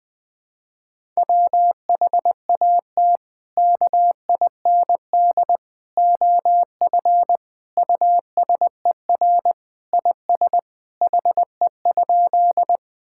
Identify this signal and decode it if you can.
Morse code, heard as a keyed tone, about 20 wpm, 700 Hz. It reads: WHAT KIND OF USER IS HE?
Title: Sentences from Top 1000 Words - Review Part 5 - 20wpm